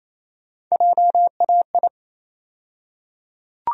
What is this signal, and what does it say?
Morse code, heard as a keyed tone, about 27 wpm, 700 Hz. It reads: JAI
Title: USA Call Sign Postfixes - 3 Letters 28wpm